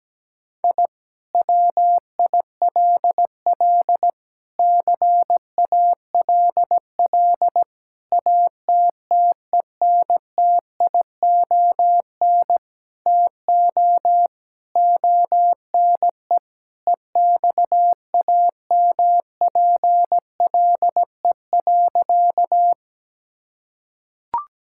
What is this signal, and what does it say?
Morse code, heard as a keyed tone, about 17 wpm, 700 Hz. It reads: I WILL CALL ATTENTION TO ONE EXAMPLE.
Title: Sentences from Top 1000 Words - Review Part 6 - 17wpm